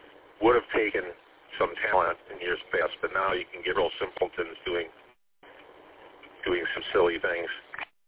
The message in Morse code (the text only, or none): none